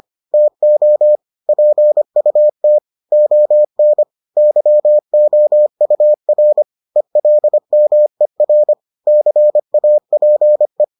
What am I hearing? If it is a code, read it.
Morse code, heard as a keyed tone, about 25 wpm, 600 Hz. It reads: TO PUT ON YOUR ELMER CAPE